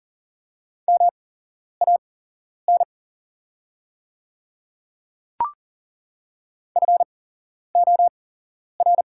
Morse code, heard as a keyed tone, about 40 wpm, spaced as for 11 wpm, 700 Hz. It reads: MAN FOR
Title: 8x Character Spacing - Top 200 Words - 40wpm